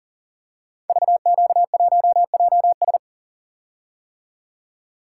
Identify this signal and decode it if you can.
Morse code, heard as a keyed tone, about 40 wpm, 700 Hz. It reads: VQ1JS
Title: Code Groups - 5 Letter:Numbers - 40wpm